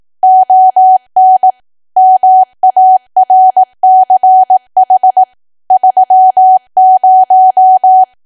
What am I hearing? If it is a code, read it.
Morse code, heard as a keyed tone, about 18 wpm, 750 Hz. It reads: ON MARCH 30
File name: cwa / W1AW-Level2 / 18WPM